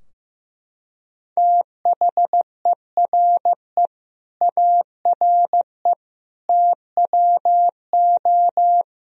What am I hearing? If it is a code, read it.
Morse code, heard as a keyed tone, about 16 wpm, 700 Hz. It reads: THERE ARE TWO
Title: Sentences from Top 1000 Words - Review Part 3 - 15wpm